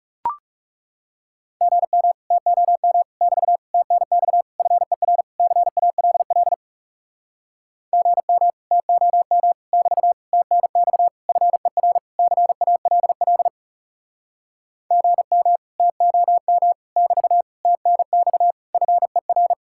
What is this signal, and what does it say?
Morse code, heard as a keyed tone, about 40 wpm, 700 Hz. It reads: GM TOM <BT> TNX FER CALL GM TOM <BT> TNX FER CALL GM TOM <BT> TNX FER